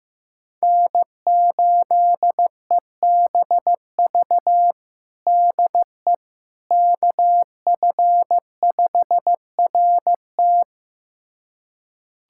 Morse code, heard as a keyed tone, about 16 wpm, 700 Hz. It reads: N8EBV DE KF5RT
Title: USA Callsign DE Callsign - 5 Characters - 15wpm